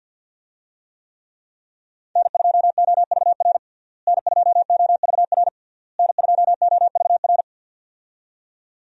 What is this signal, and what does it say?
Morse code, heard as a keyed tone, about 50 wpm, 700 Hz. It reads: N2OVR N2OVR N2OVR
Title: USA Call Signs - 5 Characters - Repeated 3x - 50wpm